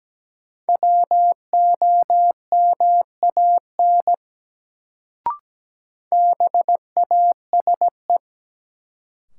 Morse code, heard as a keyed tone, about 17 wpm, 700 Hz. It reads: WOMAN BASE
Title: Sets of 5 Words - New Word Drill - 17wpm